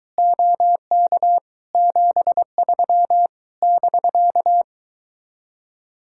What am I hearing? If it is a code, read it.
Morse code, heard as a keyed tone, about 22 wpm, 700 Hz. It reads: OK 73 <BK>